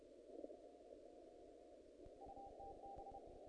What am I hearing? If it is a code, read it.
Morse code, heard as a keyed tone, about 31 wpm, 790 Hz, with noise. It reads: UTB
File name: contests / naqp-cw-k5tr-op-k5pi- -1900